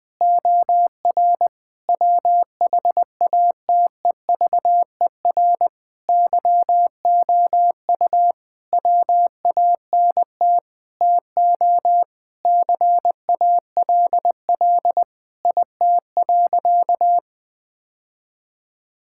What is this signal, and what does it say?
Morse code, heard as a keyed tone, about 20 wpm, 700 Hz. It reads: OR WHATEVER YOU WANT TO CALL IT.
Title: Sentences with 7-Word Phrases - Spoken Beginning - 20wpm